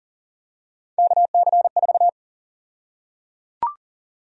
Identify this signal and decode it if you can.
Morse code, heard as a keyed tone, about 37 wpm, 700 Hz. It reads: KC4
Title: USA Call Sign Prefixes - 40wpm